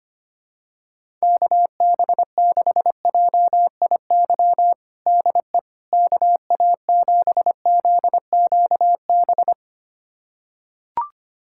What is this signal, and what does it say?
Morse code, heard as a keyed tone, about 25 wpm, 700 Hz. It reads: KB6JIY DE KA7ZQB
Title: USA Callsign DE Callsign - 6 Characters - 25wpm